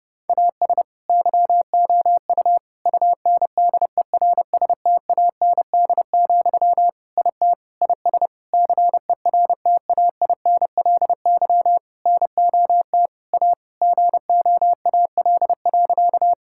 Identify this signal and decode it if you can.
Morse code, heard as a keyed tone, about 30 wpm, 700 Hz. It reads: AS YOU UNDERSTAND, IT IS CERTAINLY NOT A GOAL.
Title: Sentences from Top 1000 Words - Review Part 5 - 30wpm